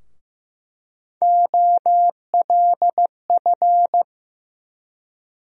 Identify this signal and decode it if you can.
Morse code, heard as a keyed tone, about 15 wpm, 700 Hz. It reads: OLF
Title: USA Call Sign Postfixes - 3 Letters 15wpm